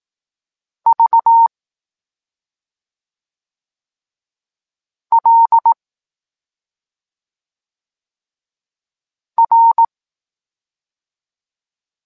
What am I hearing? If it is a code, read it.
Morse code, heard as a keyed tone, about 18 wpm, 920 Hz. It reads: V L R